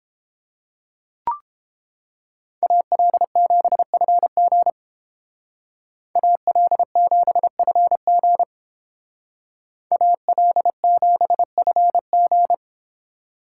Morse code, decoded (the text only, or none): AL7FG AL7FG AL7FG